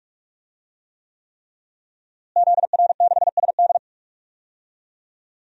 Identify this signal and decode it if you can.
Morse code, heard as a keyed tone, about 46 wpm, 700 Hz. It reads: ZR6SD
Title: DXCC-100 Call Signs - 5 Characters - Repeated 3x - 45wpm